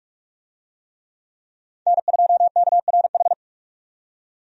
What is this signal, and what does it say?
Morse code, heard as a keyed tone, about 46 wpm, 700 Hz. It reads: N2KRH
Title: USA Call Signs - 5 Characters - 45wpm